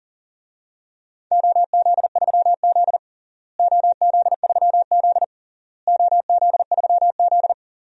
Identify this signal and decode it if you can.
Morse code, heard as a keyed tone, about 40 wpm, 700 Hz. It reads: OZ3Z OZ3Z OZ3Z
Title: DXCC-100 Call Signs - 4 Characters - Repeated 3x - 40wpm